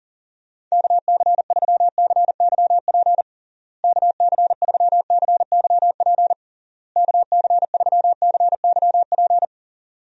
Morse code, heard as a keyed tone, about 37 wpm, 700 Hz. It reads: KC3CYP KC3CYP KC3CYP
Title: USA Call Signs - 6 Characters - Repeated 3x - 40wpm